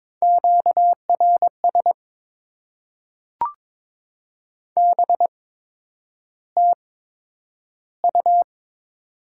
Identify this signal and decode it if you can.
Morse code, heard as a keyed tone, about 22 wpm, spaced as for 9 wpm, 700 Hz. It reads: QRS BTU
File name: Top 90 QSO Elements - 8x Character Spacing - 22wpm